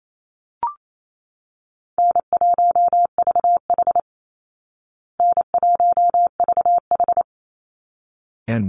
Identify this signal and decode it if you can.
Morse code, heard as a keyed tone, about 28 wpm, 700 Hz. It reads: N1VH N1VH
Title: Call Signs Repeated 3x - 28wpm